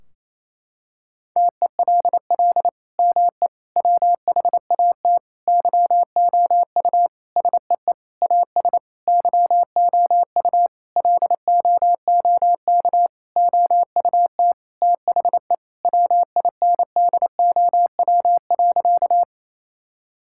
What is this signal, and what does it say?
Morse code, heard as a keyed tone, about 28 wpm, 700 Hz. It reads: TELL ME WHAT YOU SEE AS YOU LOOK OUT THE WINDOW.
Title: Sentences from Top 1600 Words 28wpm